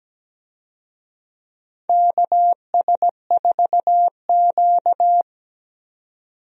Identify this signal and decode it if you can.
Morse code, heard as a keyed tone, about 17 wpm, 700 Hz. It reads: KS4Q
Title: USA Call Signs - 4 Characters - 17wpm